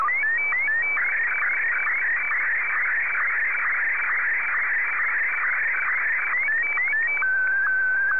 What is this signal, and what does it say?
SSTV (slow-scan television), Martin 1.
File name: sstv / M1-VK2OY